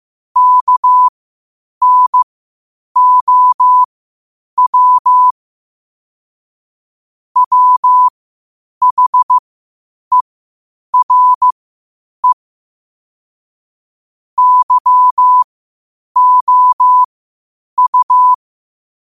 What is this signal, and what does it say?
Morse code, heard as a keyed tone, about 15 wpm, spaced as for 8 wpm, 1000 Hz. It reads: KNOW WHERE YOU
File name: qotdfiles / 8wpm